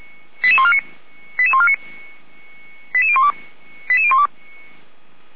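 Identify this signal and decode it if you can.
Five-tone selective call (ZVEI), heard as quick runs of five tones, tones of about 70 ms. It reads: 98814 88012 88012
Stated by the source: ZVEI five-tone selective call received on SDR.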